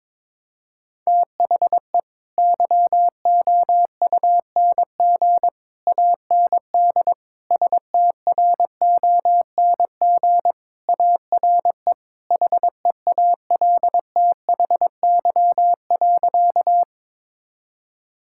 Morse code, heard as a keyed tone, about 22 wpm, 700 Hz. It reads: THE YOUNG AND STRONG ARE HEALTHY.
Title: Sentences from Top 1000 Words - Review Part 5 - 22wpm